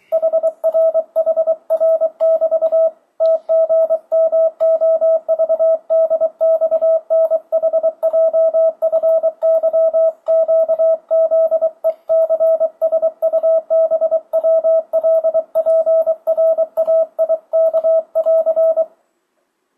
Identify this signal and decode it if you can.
Morse code, heard as a keyed tone, about 23 wpm, 650 Hz, with radio noise. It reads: HR HR <BT> TGMOVDXNHJFYQZECSUBWLPRAIK <AR>
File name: 2025cw / speedup / 24wpm